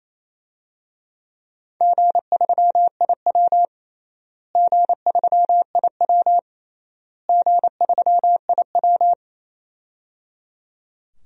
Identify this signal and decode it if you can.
Morse code, heard as a keyed tone, about 28 wpm, 700 Hz. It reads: G3IW G3IW G3IW
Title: DXCC-100 Call Signs - 4 Characters - Repeated 3x - 28wpm